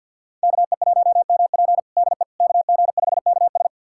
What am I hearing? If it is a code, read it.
Morse code, heard as a keyed tone, about 51 wpm, 700 Hz. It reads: KE1MP DE KG5KS
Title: USA Callsign DE Callsign - 5 Characters - 50wpm